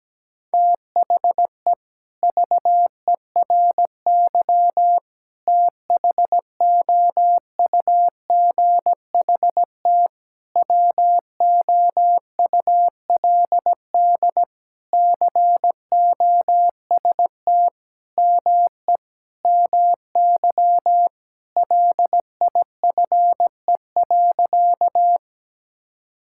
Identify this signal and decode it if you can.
Morse code, heard as a keyed tone, about 17 wpm, 700 Hz. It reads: THE VERY THOUGHT WOULD COST ME MY LIFE.
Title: Sentences from Top 1000 Words - Review Part 5 - 17wpm